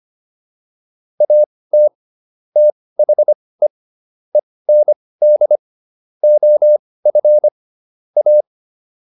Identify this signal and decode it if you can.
Morse code, heard as a keyed tone, about 25 wpm, spaced as for 18 wpm, 600 Hz. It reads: AT THE END OF A